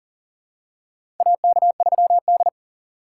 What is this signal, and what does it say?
Morse code, heard as a keyed tone, about 40 wpm, 700 Hz. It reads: AK3D
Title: USA Call Signs - 4 Characters - 4x Character Spacing - 40wpm